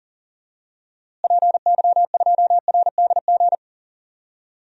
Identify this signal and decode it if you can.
Morse code, heard as a keyed tone, about 40 wpm, 700 Hz. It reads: PY2RDG
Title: DXCC-100 Call Signs - 6 Characters - Repeated 3x - 40wpm